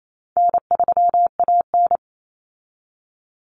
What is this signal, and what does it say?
Morse code, heard as a keyed tone, about 28 wpm, 700 Hz. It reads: N3AN
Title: Common Cut Numbers - 4-Digit 28wpm